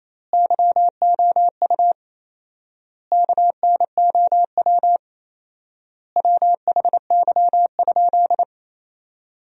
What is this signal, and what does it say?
Morse code, heard as a keyed tone, about 28 wpm, 700 Hz. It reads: YOU KNOW WHY?
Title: 4x Word-Spacing - Sentences from Top 200 Words - 28wpm